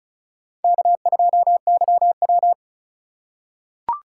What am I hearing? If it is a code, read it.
Morse code, heard as a keyed tone, about 34 wpm, 700 Hz. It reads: K2YW
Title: USA Call Signs - 4 Characters - 3x Character Spacing - 35wpm